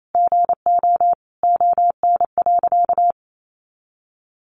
Morse code, heard as a keyed tone, about 28 wpm, 700 Hz. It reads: GO ON.
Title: Sentences from Top 1300 Words 28wpm